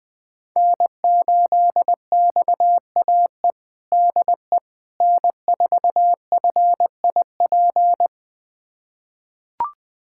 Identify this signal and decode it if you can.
Morse code, heard as a keyed tone, about 20 wpm, 700 Hz. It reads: N8XAE DE N4FIP
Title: USA Callsign DE Callsign - 5 Characters - 20wpm